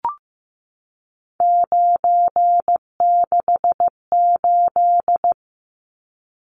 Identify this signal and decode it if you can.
Morse code, heard as a keyed tone, about 15 wpm, 700 Hz. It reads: 968
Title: Numbers - 3-Digit 15wpm